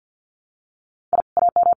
Morse code, heard as a keyed tone, about 50 wpm, 700 Hz. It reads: I AM
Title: Common QSO Phrases Encore 50wpm